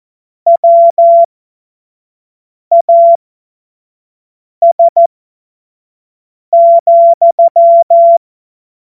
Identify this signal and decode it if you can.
Morse code, heard as a keyed tone, about 14 wpm, spaced as for 5 wpm, 680 Hz. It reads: WAS,
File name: Morsekurs / Witze / Witz 06 Februar 2024 25 F70